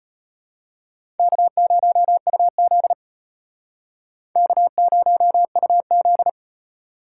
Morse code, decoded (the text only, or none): K0UZ K0UZ